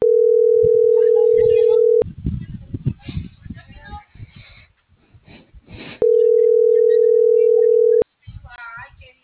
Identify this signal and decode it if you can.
Ringback tone (North America), 440 + 480 Hz.